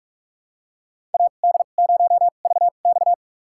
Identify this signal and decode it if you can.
Morse code, heard as a keyed tone, about 44 wpm, spaced as for 32 wpm, 700 Hz. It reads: AD0VX
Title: USA Call Signs - 5 Characters - 2x Character Spacing - 45wpm